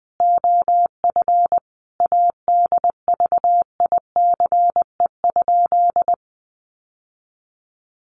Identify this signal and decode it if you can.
Morse code, heard as a keyed tone, about 20 wpm, 700 Hz. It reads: OF ADVICE?
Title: Sentences from Top 1100 Words 20wpm